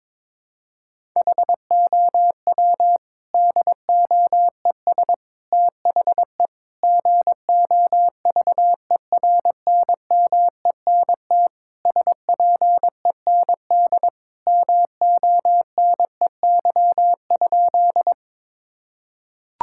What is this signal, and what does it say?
Morse code, heard as a keyed tone, about 22 wpm, 700 Hz. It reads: HOW DOES THE GOVERNMENT SPEND MONEY?
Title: Sentences from Top 1000 Words - Review Part 5 - 22wpm